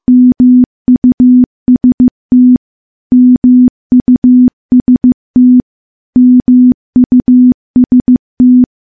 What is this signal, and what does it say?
Morse code, heard as a keyed tone, about 15 wpm, 250 Hz. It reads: MUST MUST MUST
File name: morse / morse-training-15WPM